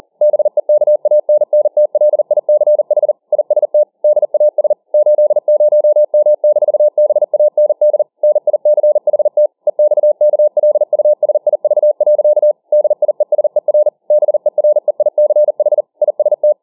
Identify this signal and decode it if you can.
Morse code, heard as a keyed tone, about 40 wpm, 600 Hz. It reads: BEKANNTLICH IST DAS 80M-BAND NICHT EXKLUSIV. DIESER BEREICH IST